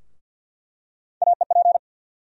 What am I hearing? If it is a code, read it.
Morse code, heard as a keyed tone, about 49 wpm, 700 Hz. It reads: AEP
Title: USA Call Sign Postfixes - 3 Letters 50wpm